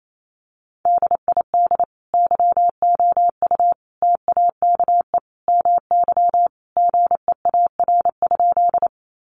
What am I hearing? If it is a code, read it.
Morse code, heard as a keyed tone, about 28 wpm, 700 Hz. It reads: DID YOU TAKE MY GEAR?
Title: Sentences from Top 1400 Words 28wpm